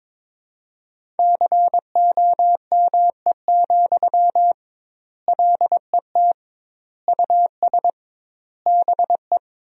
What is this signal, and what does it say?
Morse code, heard as a keyed tone, about 22 wpm, 700 Hz. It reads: COME, LET US BE
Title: 2x Word-Spacing - Sentences from Top 300 Words - 22wpm